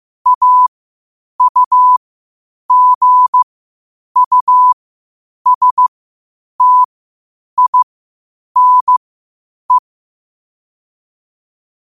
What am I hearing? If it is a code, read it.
Morse code, heard as a keyed tone, about 15 wpm, spaced as for 8 wpm, 1000 Hz. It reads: AUGUSTINE
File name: qotdfiles / 8wpm